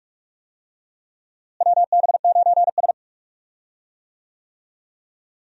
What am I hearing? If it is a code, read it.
Morse code, heard as a keyed tone, about 42 wpm, 700 Hz. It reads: WB9S
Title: USA Call Signs - 4 Characters - Repeated 2x - 45wpm